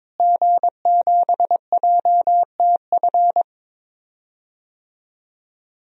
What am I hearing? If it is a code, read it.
Morse code, heard as a keyed tone, about 22 wpm, 700 Hz. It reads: G7JTF
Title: DXCC-100 Call Signs - 5 Characters - Repeated 3x - 22wpm